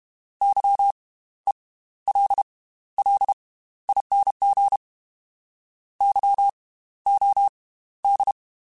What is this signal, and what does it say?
Morse code, heard as a keyed tone, about 32 wpm, spaced as for 15 wpm, 800 Hz. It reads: YELLING YOD